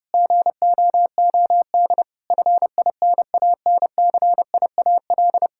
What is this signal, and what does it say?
Morse code, heard as a keyed tone, about 30 wpm, 700 Hz. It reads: GOOD FINANCIAL
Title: Sentences from Top 1000 Words - Review Part 5 - 30wpm